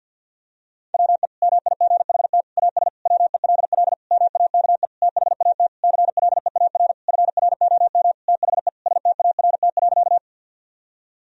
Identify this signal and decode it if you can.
Morse code, heard as a keyed tone, about 51 wpm, 700 Hz. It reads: WE MIGHT AS WELL MAKE THAT CLEAR FROM THE START.
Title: Sentences from Top 1000 Words - Review Part 1 - 50wpm